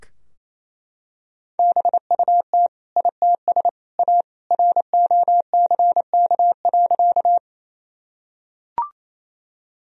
Morse code, heard as a keyed tone, about 28 wpm, 700 Hz. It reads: BUT ITS A ROCK.
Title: Sentences from Top 1000 Words - Review Part 2 - 28wpm